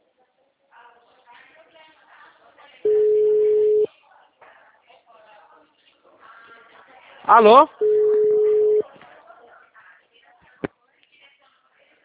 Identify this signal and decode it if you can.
Ringback tone (Europe, 425 Hz), with speech.